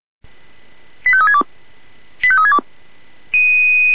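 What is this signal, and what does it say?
Five-tone selective call (ZVEI), heard as quick runs of five tones, tones of about 70 ms. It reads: 85362 85362
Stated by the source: Received via SDR as a ZVEI five-tone call, address 85362.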